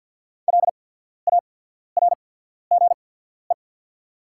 Morse code, heard as a keyed tone, about 48 wpm, spaced as for 14 wpm, 700 Hz. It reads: LARGE
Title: 8x Character Spacing - Top 500 Words - 50wpm